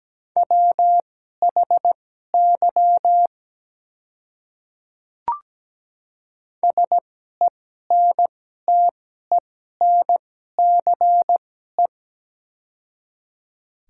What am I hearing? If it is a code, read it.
Morse code, heard as a keyed tone, about 17 wpm, spaced as for 12 wpm, 700 Hz. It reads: WHY SENTENCE
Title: 2x Character Spacing - Top 200 Words - 17wpm